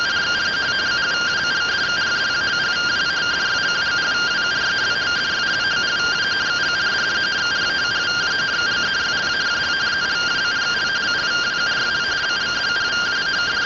Two-tone teleprinter signal, about 170 Hz shift.